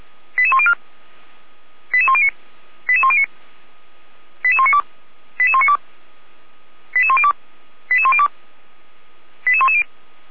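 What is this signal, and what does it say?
Five-tone selective call (ZVEI), heard as quick runs of five tones, tones of about 70 ms. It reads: 88104 88109 88109 88192 88192 88102 88102 88110
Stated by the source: SDR receiver recording of ZVEI five-tone selective call, code 88103.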